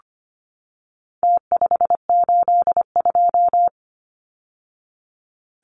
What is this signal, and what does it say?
Morse code, heard as a keyed tone, about 25 wpm, 700 Hz. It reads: T582
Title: Numbers - Contest Style - 4-Digit 25wpm